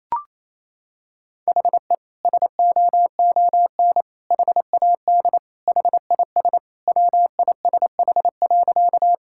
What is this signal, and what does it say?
Morse code, heard as a keyed tone, about 28 wpm, 700 Hz. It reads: HE SOON HAD HIS WISH.
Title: Sentences from Top 1600 Words 28wpm